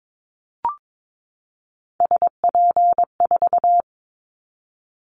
Morse code, heard as a keyed tone, about 22 wpm, 700 Hz. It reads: SP4
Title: Callsign Prefixes for Top 25 DXCC - 22wpm